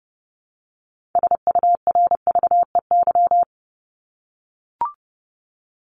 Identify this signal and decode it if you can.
Morse code, heard as a keyed tone, about 30 wpm, 700 Hz. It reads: SURVEY
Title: Top 1501-1750 Words 30wpm